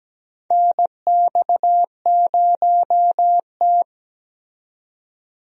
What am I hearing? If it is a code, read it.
Morse code, heard as a keyed tone, about 17 wpm, 700 Hz. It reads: NX0T
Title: USA Call Signs - 4 Characters - 17wpm